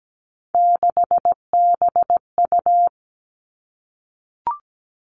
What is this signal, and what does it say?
Morse code, heard as a keyed tone, about 17 wpm, 700 Hz. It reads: 6BU
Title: Three Challenging Letter-Number Combinations 17wpm